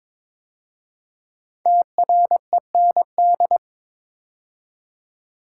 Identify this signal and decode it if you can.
Morse code, heard as a keyed tone, about 23 wpm, 700 Hz. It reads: TREND
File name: 5 Letter Words - First 3 Letters Given - 22wpm